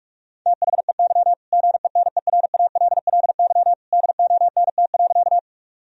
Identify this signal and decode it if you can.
Morse code, heard as a keyed tone, about 44 wpm, 700 Hz. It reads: THEY GENERALLY DONT.
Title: Sentences from Top 1000 Words - Review Part 4 - 45wpm